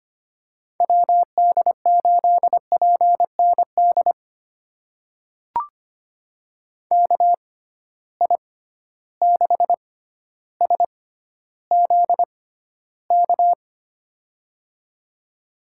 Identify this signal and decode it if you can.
Morse code, heard as a keyed tone, about 25 wpm, spaced as for 13 wpm, 700 Hz. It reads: WD8PND KI6SZK
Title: USA Call Signs - 6 Characters - 6x Character Spacing - 25wpm